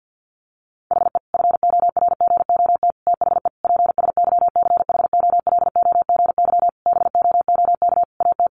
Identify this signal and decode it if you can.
Morse code, heard as a keyed tone, about 50 wpm, 700 Hz. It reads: HE FORGOT THE PSYCHOLOGY BOOK AT
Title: Sentences from Top 1400 Words 50wpm